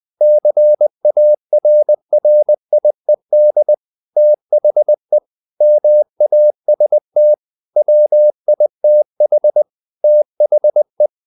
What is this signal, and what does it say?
Morse code, heard as a keyed tone, about 20 wpm, 600 Hz. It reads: CARRIED THE MAST WITH THE
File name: the old man and the sea / the old man and the sea part 1 20wpm